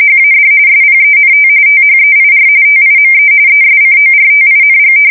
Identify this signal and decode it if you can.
RTTY, about 45.45 baud. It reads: HAD SPENT THEIR TIME ESCORTING
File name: fsk170 / battleshipnj